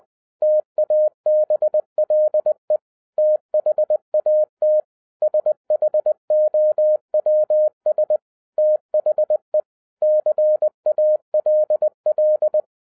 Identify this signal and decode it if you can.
Morse code, heard as a keyed tone, about 20 wpm, 610 Hz. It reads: TABLE THAT SHOWS THE CALL